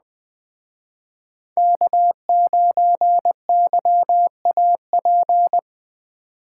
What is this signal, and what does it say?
Morse code, heard as a keyed tone, about 20 wpm, 700 Hz. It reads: K9YAP